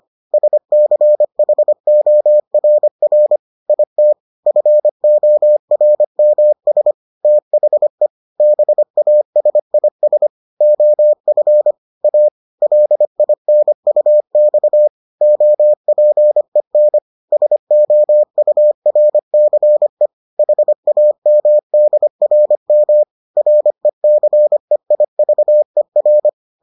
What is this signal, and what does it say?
Morse code, heard as a keyed tone, about 25 wpm, 600 Hz. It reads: SCHORR IT FORMS THE BASIS OF A LINUX OPEN SOURCE HAMDRM RECEIVER